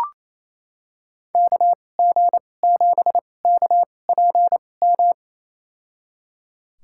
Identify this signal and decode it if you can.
Morse code, heard as a keyed tone, about 28 wpm, spaced as for 20 wpm, 700 Hz. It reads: KG7KPM
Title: USA Call Signs - 6 Characters - 2x Character Spacing - 28wpm